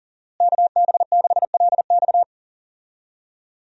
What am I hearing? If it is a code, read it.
Morse code, heard as a keyed tone, about 40 wpm, 700 Hz. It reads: KB6LX
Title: USA Call Signs - 5 Characters - Repeated 2x - 40wpm